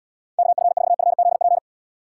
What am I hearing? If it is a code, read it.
Morse code, heard as a keyed tone, about 78 wpm, 700 Hz. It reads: KF4FZP